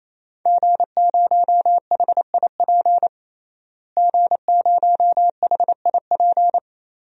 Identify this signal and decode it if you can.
Morse code, heard as a keyed tone, about 28 wpm, 700 Hz. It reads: G0HIP G0HIP
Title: DXCC-100 Call Signs - 5 Characters - Repeated 3x - 28wpm